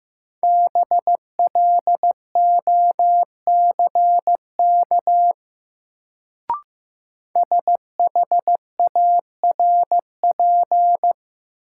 Morse code, heard as a keyed tone, about 15 wpm, 700 Hz. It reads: BLOCK SHARP
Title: Sentences from Top 1600 Words - New Word Drill - 15wpm